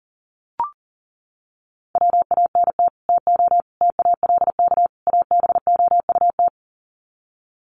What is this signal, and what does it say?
Morse code, heard as a keyed tone, about 40 wpm, 700 Hz. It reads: WANT TO TALK ABOUT
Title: Sets of 4 Words 40wpm